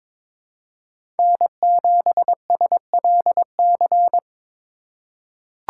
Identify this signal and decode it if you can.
Morse code, heard as a keyed tone, about 22 wpm, 700 Hz. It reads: N7SLC
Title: USA Call Signs - 5 Characters - 2x Character Spacing - 22wpm